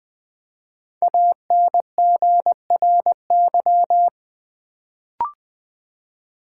Morse code, heard as a keyed tone, about 20 wpm, 700 Hz. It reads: ANGRY
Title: Build A Word - 1 Letter At A Time - 5-Letter Words - 20wpm